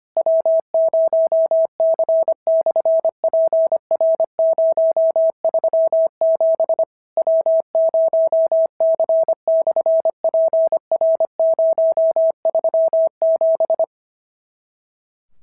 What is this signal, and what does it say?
Morse code, heard as a keyed tone, about 25 wpm, 650 Hz. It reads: W0C/PR037 W0C/PR037
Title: SOTA QSO Practice 25wpm